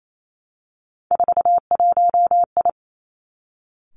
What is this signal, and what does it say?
Morse code, heard as a keyed tone, about 28 wpm, 700 Hz. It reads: 41I
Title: Three Challenging Letter-Number Combinations 28wpm